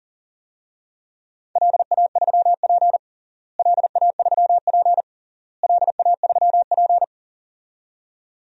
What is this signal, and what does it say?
Morse code, heard as a keyed tone, about 40 wpm, 700 Hz. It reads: LA3P LA3P LA3P
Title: DXCC-100 Call Signs - 4 Characters - Repeated 3x - 40wpm